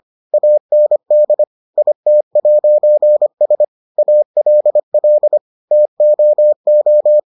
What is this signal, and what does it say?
Morse code, heard as a keyed tone, about 25 wpm, 600 Hz. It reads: AND IT'S ALL TOO